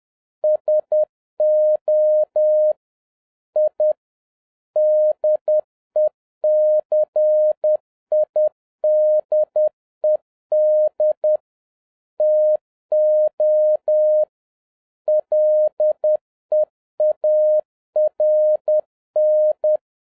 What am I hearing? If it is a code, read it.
Morse code, heard as a keyed tone, about 10 wpm, 610 Hz. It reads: SO I DECIDED TO LEARN